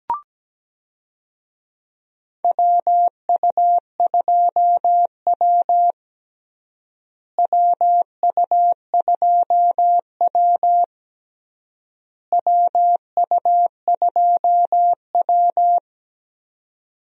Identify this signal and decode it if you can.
Morse code, heard as a keyed tone, about 17 wpm, 700 Hz. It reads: WU2W WU2W WU2W
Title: USA Call Signs - 4 Characters - Repeated 3x - 17wpm